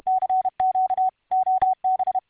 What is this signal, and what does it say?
Morse code, heard as a keyed tone, about 31 wpm, 750 Hz. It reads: CQ OB